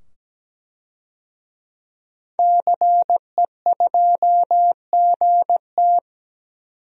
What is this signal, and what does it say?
Morse code, heard as a keyed tone, about 17 wpm, 700 Hz. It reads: CE2GT
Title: DXCC-100 Call Signs - 5 Characters - 17wpm